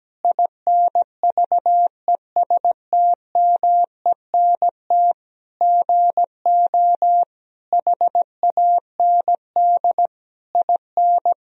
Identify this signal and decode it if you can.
Morse code, heard as a keyed tone, about 17 wpm, 700 Hz. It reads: INVESTMENT GO HAND IN
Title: Sentences from Top 1000 Words - Review Part 4 - 17wpm